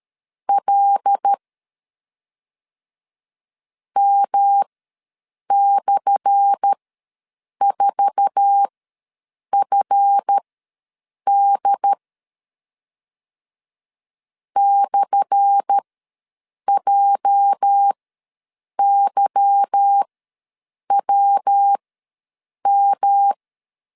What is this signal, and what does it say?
Morse code, heard as a keyed tone, about 13 wpm, spaced as for 7 wpm, 800 Hz. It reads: L M/4FD /JYWM
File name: LEZIONI / 10 SIGNS / 10A SLASH ws 12wpm 3443